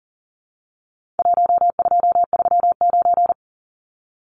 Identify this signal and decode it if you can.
Morse code, heard as a keyed tone, about 38 wpm, 700 Hz. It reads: 1239
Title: Numbers - 4-Digit 40wpm